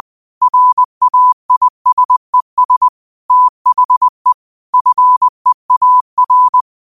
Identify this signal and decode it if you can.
Morse code, heard as a keyed tone, about 20 wpm, 1000 Hz. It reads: RAISES THE FEAR